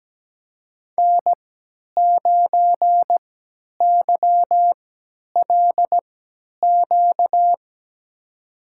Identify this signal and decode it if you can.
Morse code, heard as a keyed tone, about 17 wpm, spaced as for 10 wpm, 700 Hz. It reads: N9YLQ
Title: USA Call Signs - 5 Characters - 3x Character Spacing - 17wpm